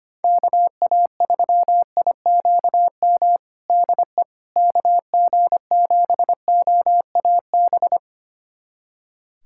Morse code, heard as a keyed tone, about 25 wpm, 700 Hz. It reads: KA3IQM DE KG7OAB
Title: USA Callsign DE Callsign - 6 Characters - 25wpm